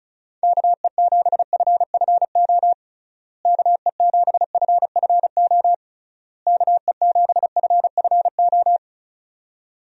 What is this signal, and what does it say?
Morse code, heard as a keyed tone, about 36 wpm, 700 Hz. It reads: KE7FFO KE7FFO KE7FFO
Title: USA Call Signs - 6 Characters - Repeated 3x - 35wpm